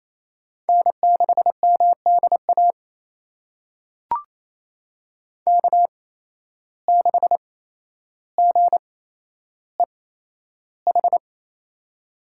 Morse code, heard as a keyed tone, about 28 wpm, spaced as for 12 wpm, 700 Hz. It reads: N6MDA K6GEH
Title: USA Call Signs - 5 Characters - 8x Character Spacing - 28wpm